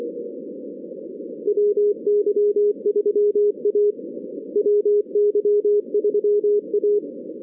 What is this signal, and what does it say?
Morse code, heard as a keyed tone, about 25 wpm, 410 Hz. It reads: WY3A WY3A
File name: 2016 arrl dx cw / 2016-ARRL-DX-CW-K5TR- - 0-22